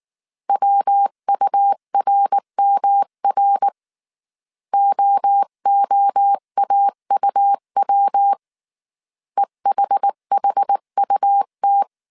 Morse code, heard as a keyed tone, about 19 wpm, 800 Hz. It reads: WURMR OOAUW EHHUT
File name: CW / DET / 1 3 1500C S B 18wpm 1916